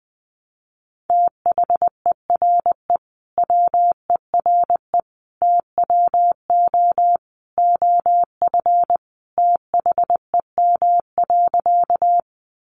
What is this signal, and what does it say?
Morse code, heard as a keyed tone, about 20 wpm, 700 Hz. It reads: THERE WERE TWO OF THEM.
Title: Sentences from Top 100 Words 20wpm